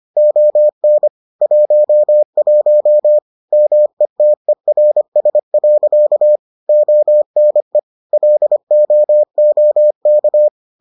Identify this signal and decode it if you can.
Morse code, heard as a keyed tone, about 25 wpm, 600 Hz. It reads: ON 11 METERS. ONE LOOK